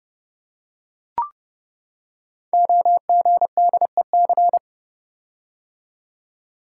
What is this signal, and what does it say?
Morse code, heard as a keyed tone, about 29 wpm, 700 Hz. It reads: OGDEC
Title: Code Groups - 5 Letters - 30wpm